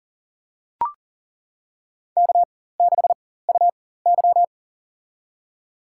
Morse code, heard as a keyed tone, about 40 wpm, spaced as for 19 wpm, 700 Hz. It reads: K6UY
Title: USA Call Signs - 4x Character Spacing - 40wpm